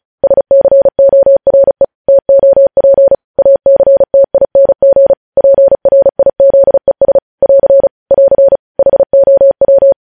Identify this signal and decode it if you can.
Morse code, heard as a keyed tone, about 36 wpm, 550 Hz. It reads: SCORE TOP ACTING PRIZES <AR> <AR> HOW
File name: podcast / News headlines-1x-35x35